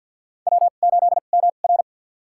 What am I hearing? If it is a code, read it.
Morse code, heard as a keyed tone, about 52 wpm, spaced as for 37 wpm, 700 Hz. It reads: W8MR
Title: USA Call Signs - 4 Characters - 2x Character Spacing - 50wpm